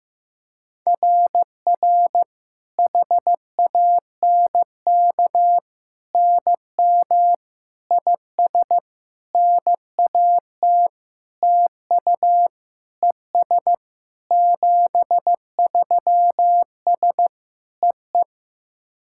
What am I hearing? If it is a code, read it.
Morse code, heard as a keyed tone, about 15 wpm, 700 Hz. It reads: RR HANK NM IS NAT TU ES 73S EE
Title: SOTA QSO Practice 15wpm